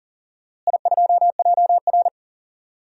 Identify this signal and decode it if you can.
Morse code, heard as a keyed tone, about 40 wpm, 700 Hz. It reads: I2JR
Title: DXCC-100 Call Signs - 4 Characters - Repeated 2x - 40wpm